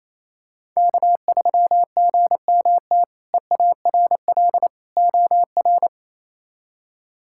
Morse code, heard as a keyed tone, about 27 wpm, 700 Hz. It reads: K3GMT EARL OR
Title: K1USN SST QSO Practice - 28wpm